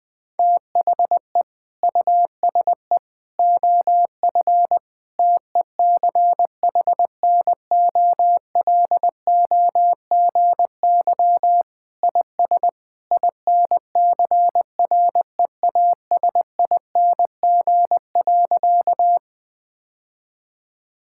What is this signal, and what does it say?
Morse code, heard as a keyed tone, about 20 wpm, 700 Hz. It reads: THE USE OF TECHNOLOGY IS INCREASING.
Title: Sentences with 2-Word Phrases - 20wpm